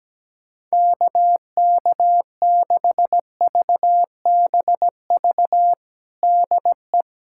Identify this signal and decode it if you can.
Morse code, heard as a keyed tone, about 17 wpm, 700 Hz. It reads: KK6VBV DE
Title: USA Callsign DE Callsign - 6 Characters - 17wpm